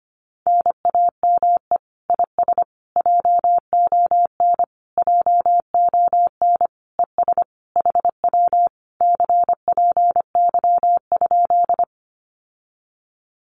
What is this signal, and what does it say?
Morse code, heard as a keyed tone, about 25 wpm, 700 Hz. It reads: NAME IS JON JON ES HW CPY?
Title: Common QSO Phrases Encore 25wpm